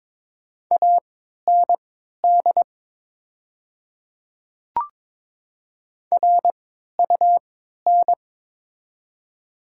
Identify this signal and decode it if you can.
Morse code, heard as a keyed tone, about 22 wpm, spaced as for 13 wpm, 700 Hz. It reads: AND RUN
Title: 3x Character Spacing - Top 500 Words - 22wpm